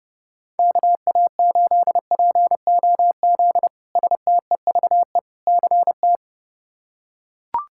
Morse code, heard as a keyed tone, about 32 wpm, 700 Hz. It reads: KA8POZ STEVE CT
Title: K1USN SST QSO Practice - 30wpm